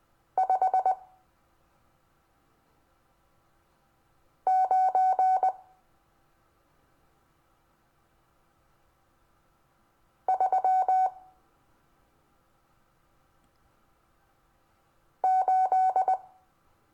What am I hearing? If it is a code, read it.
Morse code, heard as a keyed tone, about 19 wpm, 750 Hz. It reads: 5 9 3 8
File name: cw / Lesson26 SUMMARY